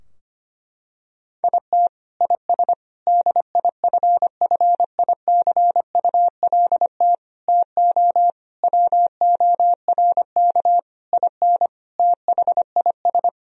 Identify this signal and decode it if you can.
Morse code, heard as a keyed tone, about 25 wpm, 700 Hz. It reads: IT IS DIFFICULT TO WORK IN THIS
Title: Sentences from Top 1000 Words - Review Part 3 - 25wpm